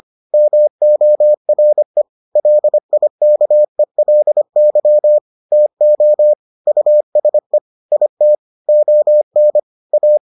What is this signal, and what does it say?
Morse code, heard as a keyed tone, about 25 wpm, 600 Hz. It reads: MORE LIKELY TO USE IT ON A